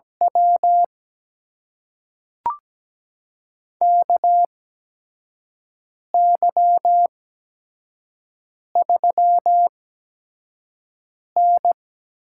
Morse code, heard as a keyed tone, about 17 wpm, spaced as for 5 wpm, 700 Hz. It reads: W KY3N
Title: USA Call Signs - 8x Character Spacing - 17wpm